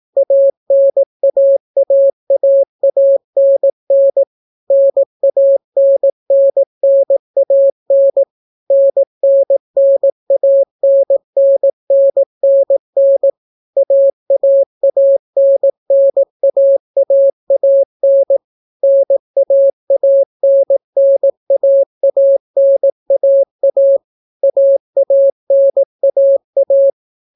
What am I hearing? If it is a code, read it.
Morse code, heard as a keyed tone, about 18 wpm, 550 Hz. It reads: ANAAAANN NANNNAN NNNANNNNN AAANNAAAN NAANNAANAA AANAA